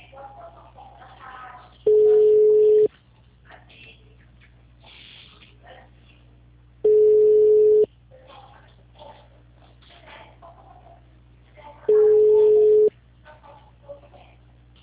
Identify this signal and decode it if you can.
Ringback tone (Europe, 425 Hz).
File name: out- -2050- - - .35824